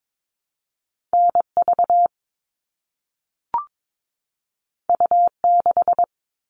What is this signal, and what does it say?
Morse code, heard as a keyed tone, about 22 wpm, 700 Hz. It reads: NV U6
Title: All Cut Numbers - 2-Digit 22wpm